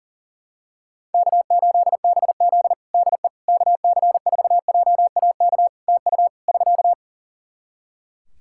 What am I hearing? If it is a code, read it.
Morse code, heard as a keyed tone, about 37 wpm, 700 Hz. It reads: K8BZ DE KC4JAK TU <SK>
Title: Practice QSO - Exchange 4 - Ending - 40wpm